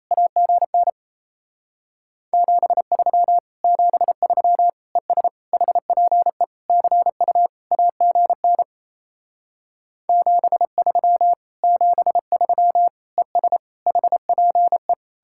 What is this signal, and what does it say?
Morse code, heard as a keyed tone, about 36 wpm, 700 Hz. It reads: AGN 73 73 ES HPE CU AGN 73 73 ES HPE